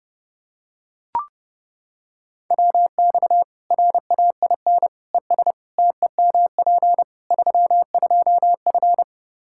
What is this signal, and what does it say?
Morse code, heard as a keyed tone, about 30 wpm, 700 Hz. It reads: WX RAIN ES TEMP 32F
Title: K1USN SST QSO Practice 30wpm